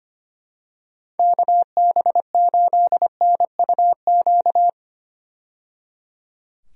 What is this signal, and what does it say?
Morse code, heard as a keyed tone, about 25 wpm, 700 Hz. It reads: KB8NUQ
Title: USA Call Signs - 6 Characters - 25wpm